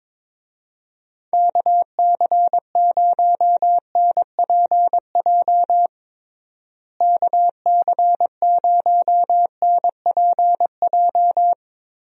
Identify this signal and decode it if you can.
Morse code, heard as a keyed tone, about 22 wpm, 700 Hz. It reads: KC0NPJ KC0NPJ
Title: USA Call Signs - 6 Characters - Repeated 3x - 22wpm